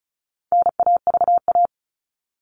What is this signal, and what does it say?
Morse code, heard as a keyed tone, about 34 wpm, 700 Hz. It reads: NAVA
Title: All Cut Numbers - 4-Digit 35wpm